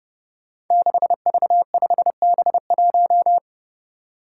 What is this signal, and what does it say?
Morse code, heard as a keyed tone, about 30 wpm, 700 Hz. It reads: 6V5B1
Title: Five Challenging Letter-Number Combinations 30wpm